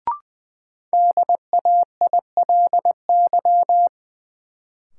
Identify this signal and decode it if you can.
Morse code, heard as a keyed tone, about 20 wpm, 700 Hz. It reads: DAILY
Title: Sentences from Top 1100 Words - New Word Drill - 20wpm